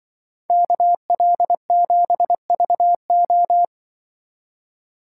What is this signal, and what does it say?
Morse code, heard as a keyed tone, about 24 wpm, 700 Hz. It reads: KL7VO
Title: Speed Racing - USA 2x2 Call Signs 20wpm